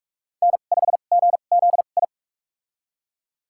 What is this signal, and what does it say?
Morse code, heard as a keyed tone, about 46 wpm, spaced as for 32 wpm, 690 Hz. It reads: N5GZI
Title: USA Call Signs - 5 Characters - 2x Character Spacing - 45wpm